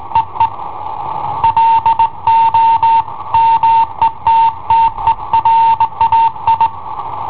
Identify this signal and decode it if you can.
Morse code, heard as a keyed tone, about 18 wpm, 910 Hz. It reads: EE LO METTERAI
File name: i7ohp / cassetta i7ohp B